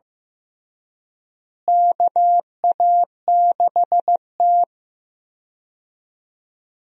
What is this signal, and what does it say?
Morse code, heard as a keyed tone, about 15 wpm, 700 Hz. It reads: KA6T
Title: USA Call Signs - 4 Characters - Repeated 2x - 15wpm